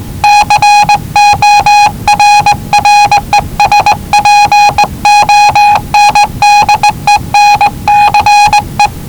Morse code, heard as a keyed tone, about 19 wpm, 840 Hz. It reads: CORRESPONDENCE
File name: WellingtonRadioZlwCloseDownOnHfAndMf / 01-Zlw 4277khz 417.5 qrt until 0925z